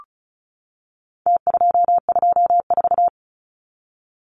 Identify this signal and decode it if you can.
Morse code, heard as a keyed tone, about 34 wpm, 700 Hz. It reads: T224
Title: Numbers - Contest Style - 4-Digit 35wpm